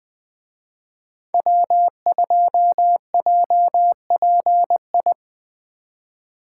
Morse code, heard as a keyed tone, about 21 wpm, 700 Hz. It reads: W2JPI